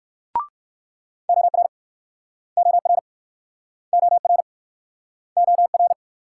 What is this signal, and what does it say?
Morse code, heard as a keyed tone, about 62 wpm, 690 Hz. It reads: OR OR OR OR